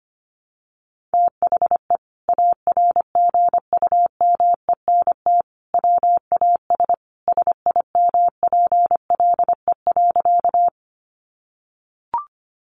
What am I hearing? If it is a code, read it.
Morse code, heard as a keyed tone, about 25 wpm, 700 Hz. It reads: THE ARGUMENT WAS SIMPLE.
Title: Sentences from Top 1400 Words 25wpm